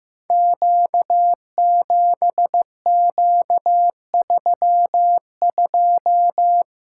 Morse code, heard as a keyed tone, about 15 wpm, 700 Hz. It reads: Q7Q32
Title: Code Groups - 5 Letter:Numbers - 15wpm